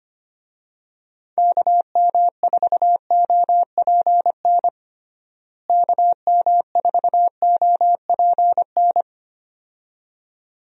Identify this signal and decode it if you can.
Morse code, heard as a keyed tone, about 25 wpm, 700 Hz. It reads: KM4OPN KM4OPN
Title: USA Call Signs - 6 Characters - Repeated 2x - 25wpm